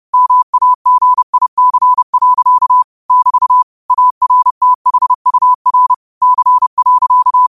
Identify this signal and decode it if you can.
Morse code, heard as a keyed tone, about 30 wpm, 1000 Hz. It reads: MAGIC. <BT> ARTHUR C.